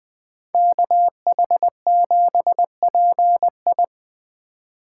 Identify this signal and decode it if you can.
Morse code, heard as a keyed tone, about 24 wpm, 700 Hz. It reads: KH7PI
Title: Speed Racing - USA 2x2 Call Signs 20wpm